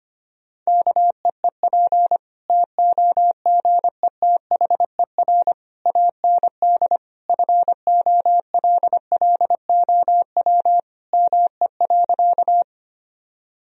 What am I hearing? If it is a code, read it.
Morse code, heard as a keyed tone, about 25 wpm, 700 Hz. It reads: KEEP TOGETHER AND FOLLOW ME.
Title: Sentences from Top 1000 Words - Review Part 1 - 25wpm